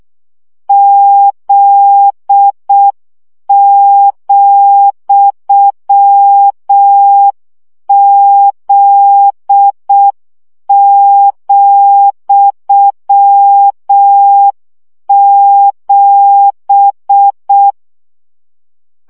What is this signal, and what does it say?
Morse code, heard as a keyed tone, about 6 wpm, 800 Hz. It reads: Z,Z,7
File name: CW gyakorlas z 7 vesszo 5 baud